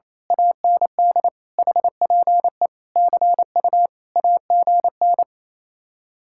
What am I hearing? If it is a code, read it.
Morse code, heard as a keyed tone, about 26 wpm, 700 Hz. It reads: AND HPE CU AGN
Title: Speed-Racing Common QSO Phrases - 28wpm